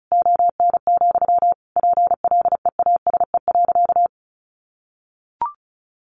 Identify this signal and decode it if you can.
Morse code, heard as a keyed tone, about 34 wpm, 700 Hz. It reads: ON, PLEASE.
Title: Sentences from Top 1500 Words 35wpm